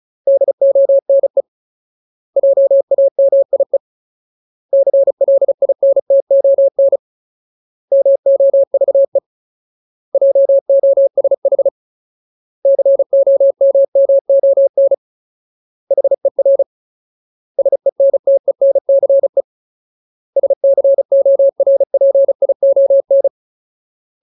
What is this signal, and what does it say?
Morse code, heard as a keyed tone, about 34 wpm, 550 Hz. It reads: DONE JAMIE CLINTON MOVE JOSH COMMON HER SENTENCE SCORPION
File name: Ultimate Word List / 35WPM Ultimate